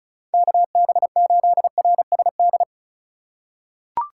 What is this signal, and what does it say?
Morse code, heard as a keyed tone, about 36 wpm, 700 Hz. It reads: KB8RSD
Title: USA Call Signs - 6 Characters - 2x Character Spacing - 35wpm